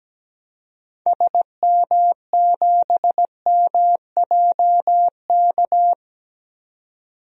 Morse code, heard as a keyed tone, about 17 wpm, 700 Hz. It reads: SM7MJK
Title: DXCC-100 Call Signs - 6 Characters - 17wpm